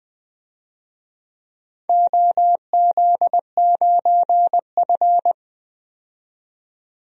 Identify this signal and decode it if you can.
Morse code, heard as a keyed tone, about 20 wpm, 700 Hz. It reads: OZ9F
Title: DXCC-100 Call Signs - 4 Characters - 20wpm